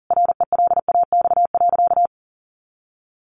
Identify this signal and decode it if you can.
Morse code, heard as a keyed tone, about 40 wpm, 700 Hz. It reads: RELAX.
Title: Sentences from Top 1400 Words 40wpm